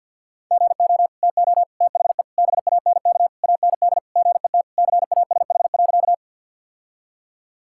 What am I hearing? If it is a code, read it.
Morse code, heard as a keyed tone, about 46 wpm, 700 Hz. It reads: GO TO THE BANK AND GET CASH.
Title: Sentences from Top 1000 Words - Review Part 5 - 50wpm